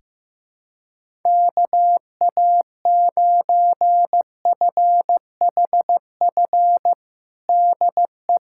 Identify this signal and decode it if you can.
Morse code, heard as a keyed tone, about 15 wpm, 700 Hz. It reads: KA9FHF DE
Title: USA Callsign DE Callsign - 6 Characters - 15wpm